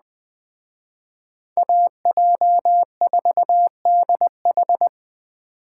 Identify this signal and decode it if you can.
Morse code, heard as a keyed tone, about 19 wpm, 700 Hz. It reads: AJ4DH